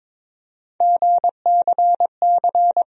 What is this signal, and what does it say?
Morse code, heard as a keyed tone, about 22 wpm, 700 Hz. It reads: GCC
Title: USA Call Sign Postfixes - 3 Letters 22wpm